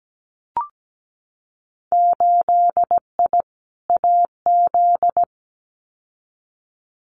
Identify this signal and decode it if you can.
Morse code, heard as a keyed tone, about 17 wpm, 700 Hz. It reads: 8I AZ
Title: Winter Field Day Exchanges Encore 17wpm